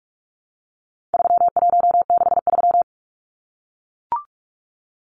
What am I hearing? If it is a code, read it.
Morse code, heard as a keyed tone, about 44 wpm, 700 Hz. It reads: 3163
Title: Numbers - 4-Digit 45wpm